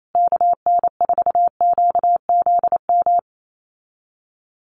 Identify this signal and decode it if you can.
Morse code, heard as a keyed tone, about 28 wpm, 700 Hz. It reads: KN4QZM
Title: USA Call Signs Repeated 3x 28wpm